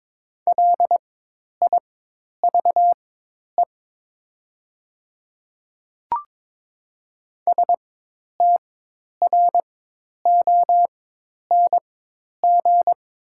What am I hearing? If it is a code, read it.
Morse code, heard as a keyed tone, about 23 wpm, spaced as for 10 wpm, 700 Hz. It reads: LIVE STRONG
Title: 4x Character Spacing - Top 500 Words - 22wpm